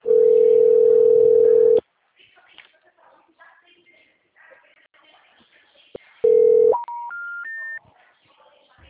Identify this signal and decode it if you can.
Ringback tone (North America), 440 + 480 Hz, with speech.
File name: out- -2046- - - .58734